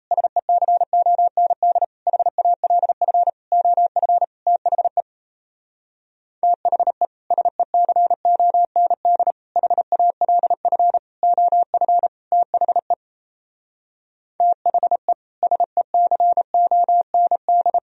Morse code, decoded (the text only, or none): SECOND HALF OF THE THE SECOND HALF OF THE THE SECOND